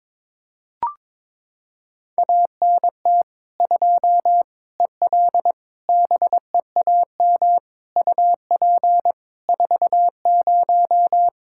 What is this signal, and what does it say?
Morse code, heard as a keyed tone, about 21 wpm, 700 Hz. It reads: ANT 2 EL BEAM UP 40
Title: K1USN SST QSO Practice 22wpm